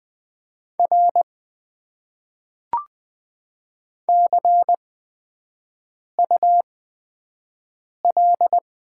Morse code, heard as a keyed tone, about 20 wpm, spaced as for 5 wpm, 700 Hz. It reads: R CUL
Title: Top 90 QSO Elements - 8x Character Spacing - 20wpm